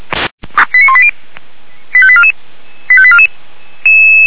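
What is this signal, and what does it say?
Five-tone selective call (ZVEI), heard as quick runs of five tones, tones of about 70 ms. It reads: 89180 85844 85844
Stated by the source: ZVEI five-tone selective call received by SDR.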